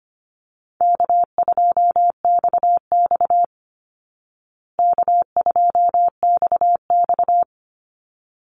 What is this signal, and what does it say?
Morse code, heard as a keyed tone, about 25 wpm, 700 Hz. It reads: K2XX K2XX
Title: Easy Call Signs Repeated 2x - 25wpm